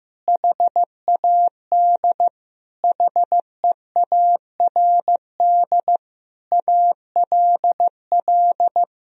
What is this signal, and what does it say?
Morse code, heard as a keyed tone, about 15 wpm, 700 Hz. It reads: HAD HEARD ALL
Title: Sentences from Top 400 Words and Abbreviations 15wpm